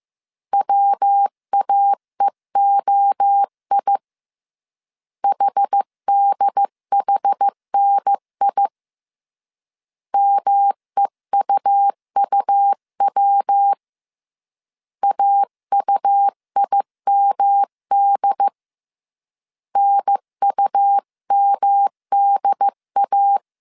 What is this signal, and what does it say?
Morse code, heard as a keyed tone, about 15 wpm, 800 Hz. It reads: WAEOI HDHNI MEUUW AUIMD NUMDA
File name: CW / DET / 1 3 1500C S A 14wpm 2447